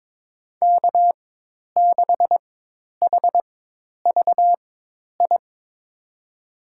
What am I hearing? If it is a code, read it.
Morse code, heard as a keyed tone, about 22 wpm, spaced as for 10 wpm, 700 Hz. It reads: K6HVI
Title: USA Call Signs - 5 Characters - 4x Character Spacing - 22wpm